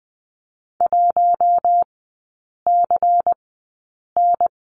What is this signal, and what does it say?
Morse code, heard as a keyed tone, about 20 wpm, spaced as for 8 wpm, 700 Hz. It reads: 1CN
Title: 100 Lesson 17e - Letter u - Three Characters 20wpm